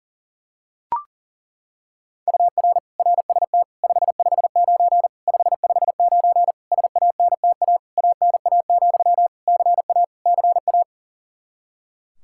Morse code, heard as a keyed tone, about 40 wpm, 700 Hz. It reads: UR RST 559 559 SANTA ANA, CA CA
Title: Practice QSO - Exchange 1 - Beginning - 40wpm